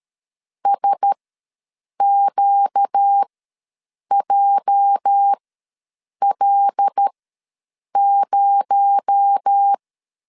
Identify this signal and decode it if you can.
Morse code, heard as a keyed tone, about 13 wpm, spaced as for 7 wpm, 800 Hz. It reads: SQJL0
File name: LEZIONI / 8 YJ45 / 8 YJ4 ws 12wpm 3249